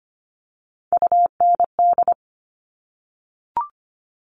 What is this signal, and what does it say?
Morse code, heard as a keyed tone, about 25 wpm, 700 Hz. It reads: UND
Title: Top Three Letter Combinations in Words 25wpm